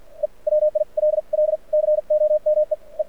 Morse code, heard as a keyed tone, about 57 wpm, spaced as for 46 wpm, 590 Hz. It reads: E WILL COME E
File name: archived / CRSnet